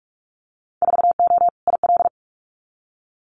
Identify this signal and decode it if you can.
Morse code, heard as a keyed tone, about 46 wpm, 700 Hz. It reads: 4O IL
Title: Winter Field Day Exchanges Encore 45wpm